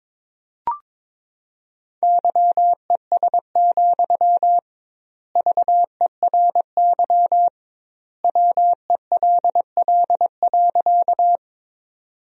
Morse code, heard as a keyed tone, about 23 wpm, 700 Hz. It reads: YES, VERY WELL.
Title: 2x Word-Spacing - Sentences from Top 500 Words - 22wpm